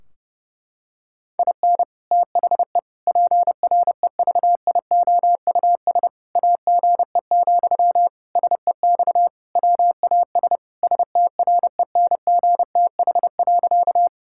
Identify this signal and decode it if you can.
Morse code, heard as a keyed tone, about 32 wpm, 700 Hz. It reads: IN THE PREVIOUS AGE, SEX WAS STRENGTH.
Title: Sentences from Top 1600 Words 30wpm